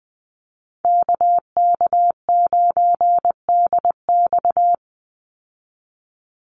Morse code, heard as a keyed tone, about 20 wpm, 700 Hz. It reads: KK9DX
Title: Call Signs Repeated 2x - 20wpm